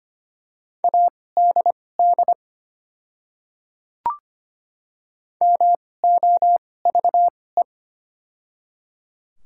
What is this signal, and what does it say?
Morse code, heard as a keyed tone, about 25 wpm, spaced as for 18 wpm, 700 Hz. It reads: ADD MOVE
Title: 2x Character Spacing - Top 200 Words - 25wpm